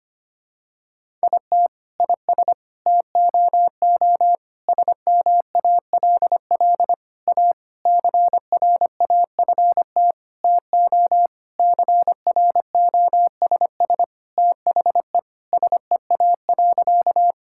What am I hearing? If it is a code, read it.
Morse code, heard as a keyed tone, about 25 wpm, 700 Hz. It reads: IT IS TOO SMALL A CRAFT TO CROSS THE SEA.
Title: Sentences from Top 1000 Words - Review Part 6 - 25wpm